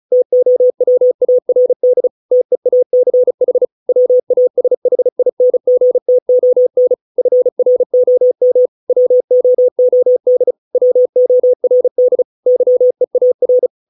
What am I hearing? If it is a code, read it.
Morse code, heard as a keyed tone, about 34 wpm, 500 Hz. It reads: TOWARD TEACH WASHINGTON FROM WOOD WORD YEAR